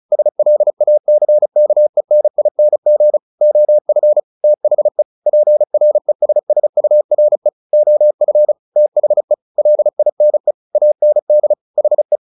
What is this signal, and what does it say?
Morse code, heard as a keyed tone, about 34 wpm, 600 Hz. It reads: SLACKENING OF THE PRESSURE OF THE LINE AND HE
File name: the old man and the sea 2 35wpm